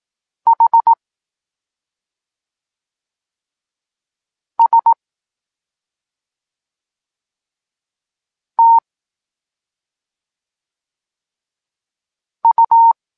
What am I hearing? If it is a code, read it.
Morse code, heard as a keyed tone, about 18 wpm, spaced as for 2 wpm, 920 Hz. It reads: HSTU
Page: Tekst 8 – doskonalenie odbioru poznanych liter